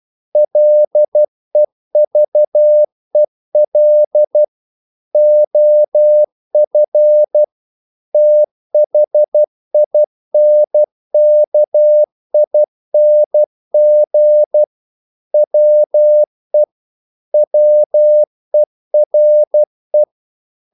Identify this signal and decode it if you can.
Morse code, heard as a keyed tone, about 12 wpm, 600 Hz. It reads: LEVEL OF THINKING WE WERE
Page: English quotations in CW (telegraphy - morse)